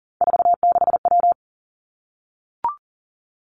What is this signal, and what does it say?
Morse code, heard as a keyed tone, about 40 wpm, 700 Hz. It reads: 46W
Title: Three Challenging Letter-Number Combinations 40wpm